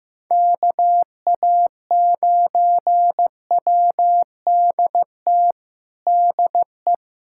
Morse code, heard as a keyed tone, about 15 wpm, 700 Hz. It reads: KA9WDT DE
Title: USA Callsign DE Callsign - 6 Characters - 15wpm